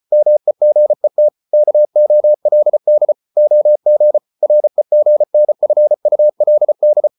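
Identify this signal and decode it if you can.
Morse code, heard as a keyed tone, about 34 wpm, 600 Hz. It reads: MEGET KOLD OG REGNFULD
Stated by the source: Morse code at 33 wpm, QRQ.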